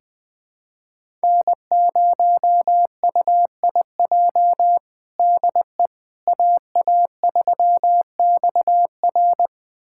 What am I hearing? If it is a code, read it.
Morse code, heard as a keyed tone, about 21 wpm, 700 Hz. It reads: N0UIJ DE AA3XR
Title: USA Callsign DE Callsign - 5 Characters - 20wpm